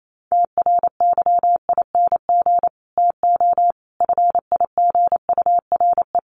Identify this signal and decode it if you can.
Morse code, heard as a keyed tone, about 28 wpm, 700 Hz. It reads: TRYING TO FIGURE
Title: Sets of 5 Words Encore 28wpm